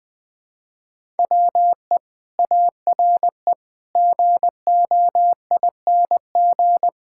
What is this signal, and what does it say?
Morse code, heard as a keyed tone, about 20 wpm, 700 Hz. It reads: WE ARE GOING
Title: Sentences with 6-Word Phrases - 20wpm